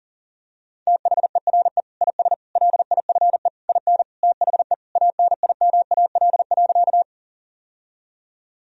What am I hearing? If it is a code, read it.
Morse code, heard as a keyed tone, about 40 wpm, 700 Hz. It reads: THERE IS LIFE IN THE ANIMAL.
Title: Sentences from Top 1000 Words - Review Part 1 - 40wpm